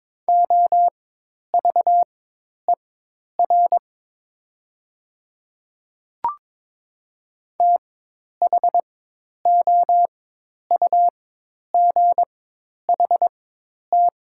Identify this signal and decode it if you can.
Morse code, heard as a keyed tone, about 22 wpm, spaced as for 10 wpm, 700 Hz. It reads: OVER THOUGHT
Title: 4x Character Spacing - Top 500 Words - 22wpm